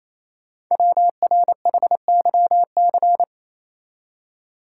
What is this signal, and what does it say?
Morse code, heard as a keyed tone, about 29 wpm, 700 Hz. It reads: WRHYC
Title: Code Groups - 5 Letters - 28wpm